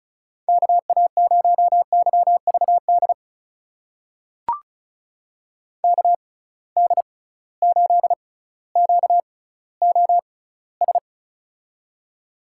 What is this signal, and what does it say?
Morse code, heard as a keyed tone, about 36 wpm, spaced as for 18 wpm, 700 Hz. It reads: KA0YVD KD8QOS
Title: USA Call Signs - 6 Characters - 6x Character Spacing - 35wpm